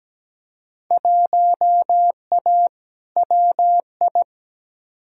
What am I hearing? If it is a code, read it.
Morse code, heard as a keyed tone, about 17 wpm, 700 Hz. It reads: 1A WI
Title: Field Day Exchanges - 17wpm